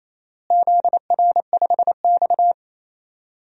Morse code, heard as a keyed tone, about 28 wpm, 700 Hz. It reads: ZR5X
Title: DXCC-100 Call Signs - 4 Characters - Repeated 3x - 28wpm